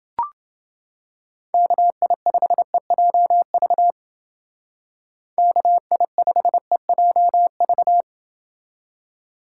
Morse code, heard as a keyed tone, about 30 wpm, 700 Hz. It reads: KI5EJV KI5EJV